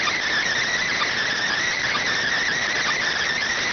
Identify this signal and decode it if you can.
Emergency Alert System (SAME) tones, heard as harsh buzzing bursts. It is heard from 0.7 to 3.1 s.